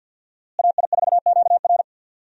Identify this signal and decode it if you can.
Morse code, heard as a keyed tone, about 51 wpm, 700 Hz. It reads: AI4QR
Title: USA Call Signs - 5 Characters - Repeated 2x - 50wpm